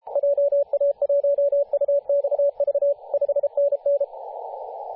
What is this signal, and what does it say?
Morse code, heard as a keyed tone, about 33 wpm, 570 Hz, with radio noise. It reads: JA1UXV 5NN
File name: posts / 2017-jan-27-0632z-a35w-14-031mhz-cw